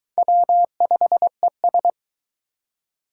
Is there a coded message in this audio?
Yes: Morse code from 0.2 to 1.9 s, heard as a keyed tone, about 22 wpm, 700 Hz. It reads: W5ES